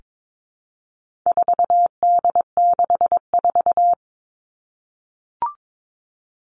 Morse code, heard as a keyed tone, about 22 wpm, 700 Hz. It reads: 4D64
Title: All Cut Numbers - 4-Digit 22wpm